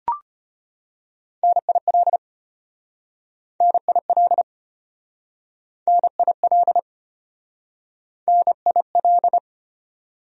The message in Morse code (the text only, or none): NIL NIL NIL NIL